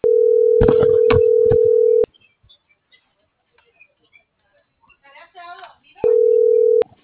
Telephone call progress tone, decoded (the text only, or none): ringback tone